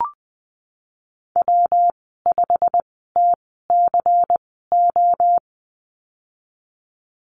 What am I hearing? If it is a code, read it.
Morse code, heard as a keyed tone, about 19 wpm, spaced as for 14 wpm, 700 Hz. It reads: W5TCO
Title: 128 Lesson 21g - Number 6 - Call Signs 20wpm